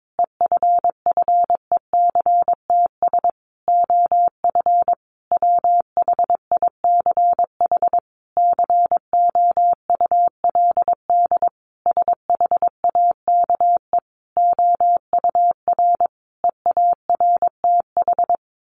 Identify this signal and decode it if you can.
Morse code, heard as a keyed tone, about 22 wpm, 700 Hz. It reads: EFFECTS OF WHICH COULD SHAKE OUR EARTH
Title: A Sound Of Thunder 22wpm